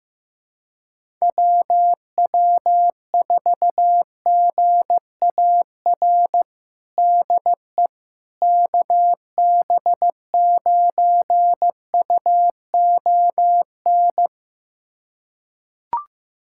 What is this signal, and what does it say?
Morse code, heard as a keyed tone, about 15 wpm, 700 Hz. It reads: WW4GAR DE KB9UON
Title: USA Callsign DE Callsign - 6 Characters - 15wpm